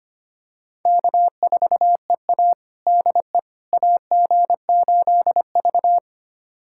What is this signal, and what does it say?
Morse code, heard as a keyed tone, about 25 wpm, 700 Hz. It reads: K4EA DE AG8V
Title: USA Callsign DE Callsign - 4 Characters - 25wpm